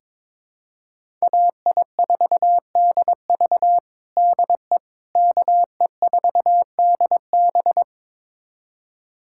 Morse code, heard as a keyed tone, about 22 wpm, 700 Hz. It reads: AI4DV DE KE4DB
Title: USA Callsign DE Callsign - 5 Characters - 22wpm